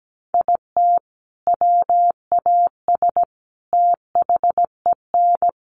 Morse code, heard as a keyed tone, about 17 wpm, 700 Hz. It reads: IT WAS THEN
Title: Sentences from Top 300 Words 17wpm